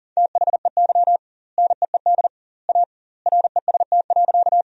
Morse code, heard as a keyed tone, about 37 wpm, 700 Hz. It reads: THEY NEED A REST.
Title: 2x Word-Spacing - Sentences from Top 500 Words - 40wpm